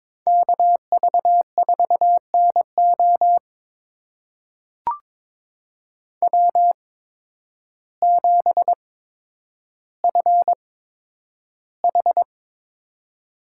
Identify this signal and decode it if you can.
Morse code, heard as a keyed tone, about 22 wpm, spaced as for 10 wpm, 700 Hz. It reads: KV4NO W7FH
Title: USA Call Signs - 5 Characters - 8x Character Spacing - 22wpm